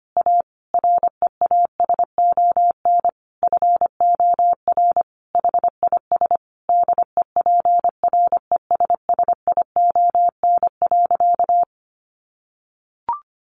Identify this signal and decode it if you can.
Morse code, heard as a keyed tone, about 25 wpm, 700 Hz. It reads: A REASON FOR HIS DEPRESSION.
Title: Sentences from Top 1300 Words 25wpm